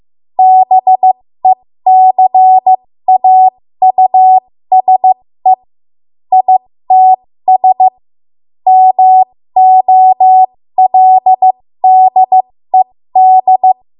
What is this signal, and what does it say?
Morse code, heard as a keyed tone, about 15 wpm, spaced as for 13 wpm, 750 Hz. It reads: BECAUSE ITS MOLDED
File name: Morse / 13WPM